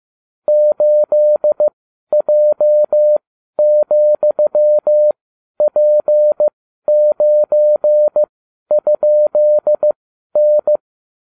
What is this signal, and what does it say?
Morse code, heard as a keyed tone, about 15 wpm, spaced as for 11 wpm, 600 Hz. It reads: 8J,P9?N